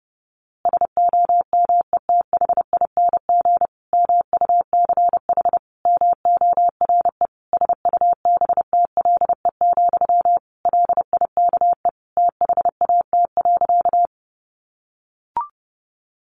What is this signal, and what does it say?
Morse code, heard as a keyed tone, about 30 wpm, 700 Hz. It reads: SOMETHING MUCH MORE SUBTLE, LIKE THAT.
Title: A Sound Of Thunder 30wpm